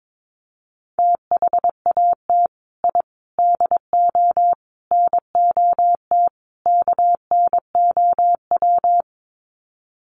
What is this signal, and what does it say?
Morse code, heard as a keyed tone, about 22 wpm, 700 Hz. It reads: THAT I DO NOT KNOW
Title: Sets of 5 Words Encore2 22wpm